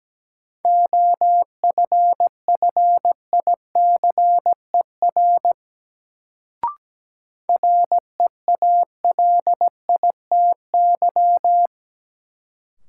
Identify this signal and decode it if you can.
Morse code, heard as a keyed tone, about 17 wpm, 700 Hz. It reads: OFFICER REALITY
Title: Sentences from Top 1200 Words - New Word Drill - 17wpm